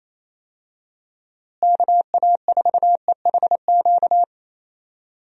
Morse code, heard as a keyed tone, about 28 wpm, 700 Hz. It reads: KA4EHQ
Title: USA Call Signs - 6 Characters - Repeated 3x - 28wpm